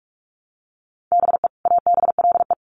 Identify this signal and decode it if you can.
Morse code, heard as a keyed tone, about 44 wpm, 700 Hz. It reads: BE ABLE
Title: Sets of 2 Words 45wpm